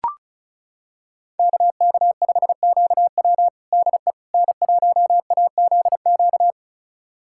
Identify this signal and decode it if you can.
Morse code, heard as a keyed tone, about 36 wpm, 700 Hz. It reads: KK5QW DE N1AZQ
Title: USA Callsign DE Callsign - 5 Characters - 35wpm